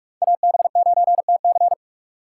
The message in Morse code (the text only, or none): AB9TC